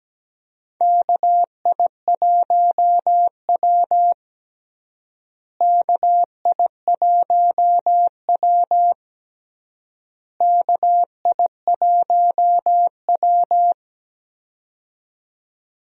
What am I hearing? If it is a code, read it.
Morse code, heard as a keyed tone, about 17 wpm, 700 Hz. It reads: KI1W KI1W KI1W
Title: USA Call Signs - 4 Characters - Repeated 3x - 17wpm